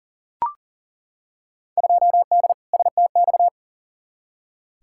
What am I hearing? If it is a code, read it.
Morse code, heard as a keyed tone, about 40 wpm, 700 Hz. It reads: 2D STX
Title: Field Day Exchanges - 40wpm